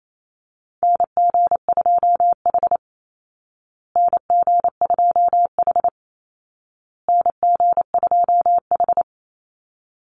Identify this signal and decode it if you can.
Morse code, heard as a keyed tone, about 28 wpm, 700 Hz. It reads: NG2H NG2H NG2H
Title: Call Signs Repeated 2x - 28wpm